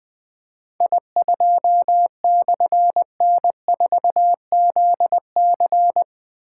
Morse code, heard as a keyed tone, about 20 wpm, 700 Hz. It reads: I2/N4ZC
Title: Difficult Call Signs 20wpm